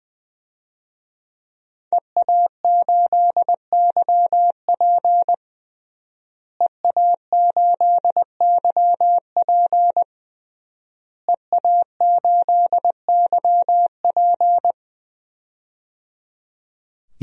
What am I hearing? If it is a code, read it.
Morse code, heard as a keyed tone, about 21 wpm, 700 Hz. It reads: EA8YP EA8YP EA8YP
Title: DXCC-100 Call Signs - 5 Characters - Repeated 3x - 20wpm